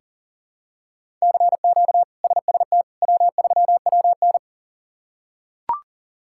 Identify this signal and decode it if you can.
Morse code, heard as a keyed tone, about 40 wpm, 700 Hz. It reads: CQ SST W3WN
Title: K1USN SST QSO Practice - 40wpm